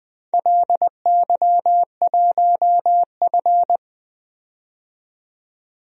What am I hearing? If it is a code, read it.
Morse code, heard as a keyed tone, about 20 wpm, 700 Hz. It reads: LY1F
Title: DXCC-100 Call Signs - 4 Characters - 20wpm